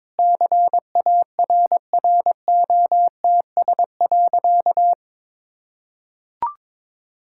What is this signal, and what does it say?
Morse code, heard as a keyed tone, about 21 wpm, 690 Hz. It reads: CARROTS.
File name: practice / 22wpm / Easy Sentences with Hints - 22wpm